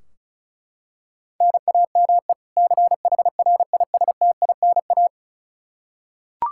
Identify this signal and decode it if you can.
Morse code, heard as a keyed tone, about 36 wpm, 700 Hz. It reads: NAME CHRISTINA
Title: Practice QSO - Exchange 1 - Beginning - 35wpm